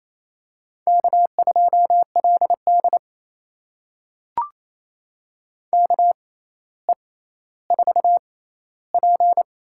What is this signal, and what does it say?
Morse code, heard as a keyed tone, about 27 wpm, spaced as for 14 wpm, 700 Hz. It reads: K2LD KE4P
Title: USA Call Signs - 4 Characters - 6x Character Spacing - 28wpm